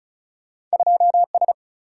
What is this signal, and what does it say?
Morse code, heard as a keyed tone, about 35 wpm, 700 Hz. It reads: 2S
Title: Single-Letter Single-Number 35wpm